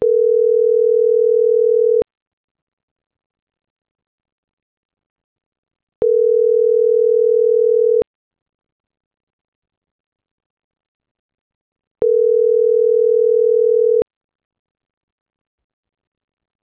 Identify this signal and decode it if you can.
Ringback tone (North America), 440 + 480 Hz.